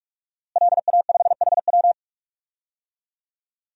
Morse code, heard as a keyed tone, about 46 wpm, 700 Hz. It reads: LA5HW
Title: DXCC-100 Call Signs - 5 Characters - Repeated 3x - 45wpm